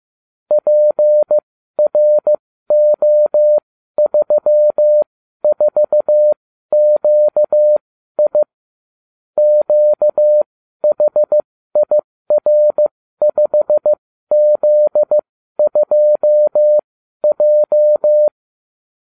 Morse code, heard as a keyed tone, about 15 wpm, spaced as for 12 wpm, 600 Hz. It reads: PRO34QI QHIR5Z2J